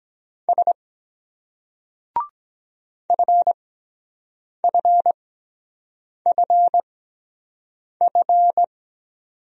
Morse code, heard as a keyed tone, about 23 wpm, spaced as for 7 wpm, 700 Hz. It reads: S FFFF